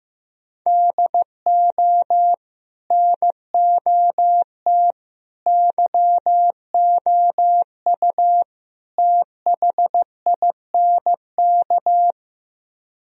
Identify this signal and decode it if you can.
Morse code, heard as a keyed tone, about 15 wpm, 700 Hz. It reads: DO NOT YOU THINK
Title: Speed-Racing - Sets of 4 Words 15wpm